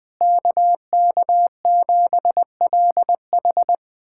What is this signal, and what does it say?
Morse code, heard as a keyed tone, about 20 wpm, 700 Hz. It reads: KK7LH
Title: USA Callsign DE Callsign - 5 Characters - 20wpm